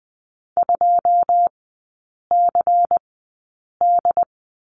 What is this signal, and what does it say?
Morse code, heard as a keyed tone, about 20 wpm, spaced as for 8 wpm, 690 Hz. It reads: 2CD
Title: 133 Lesson 22e - Question Mark - Three Characters 20wpm